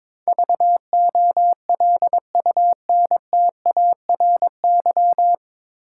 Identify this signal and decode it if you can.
Morse code, heard as a keyed tone, about 22 wpm, 700 Hz. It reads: VOLUNTARY
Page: Morse Code Ninja practice 22wpm